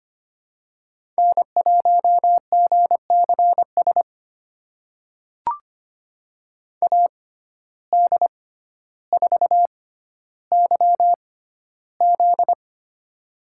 Morse code, heard as a keyed tone, about 25 wpm, spaced as for 13 wpm, 700 Hz. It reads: N1GCS AD4YZ
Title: USA Call Signs - 5 Characters - 6x Character Spacing - 25wpm